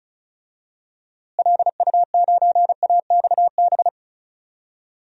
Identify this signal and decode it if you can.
Morse code, heard as a keyed tone, about 36 wpm, 700 Hz. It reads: LU9AXB
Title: DXCC-100 Call Signs - 6 Characters - Repeated 2x - 35wpm